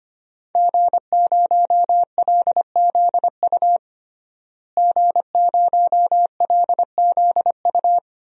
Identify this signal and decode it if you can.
Morse code, heard as a keyed tone, about 25 wpm, 700 Hz. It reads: G0LZU G0LZU
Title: DXCC-100 Call Signs - 5 Characters - Repeated 3x - 25wpm